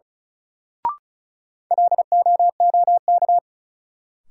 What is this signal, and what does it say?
Morse code, heard as a keyed tone, about 34 wpm, 700 Hz. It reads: LOOK
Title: Sentences from Top 100 Words - New Word Drill - 35wpm